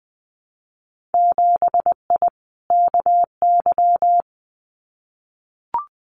Morse code, heard as a keyed tone, about 20 wpm, 700 Hz. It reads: 7I KY
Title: Winter Field Day Exchanges Encore 20wpm